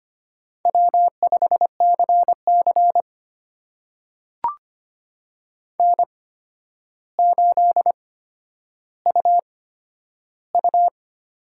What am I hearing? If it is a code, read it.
Morse code, heard as a keyed tone, about 25 wpm, spaced as for 11 wpm, 700 Hz. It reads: W5CC N8UU
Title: USA Call Signs - 8x Character Spacing - 25wpm